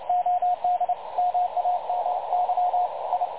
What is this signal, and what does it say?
Morse code, heard as a keyed tone, about 28 wpm, 720 Hz, with radio noise. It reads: OD MAMY I
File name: dlazaawansowanych / nagraniazpasma / websdr recording 3565.0kHz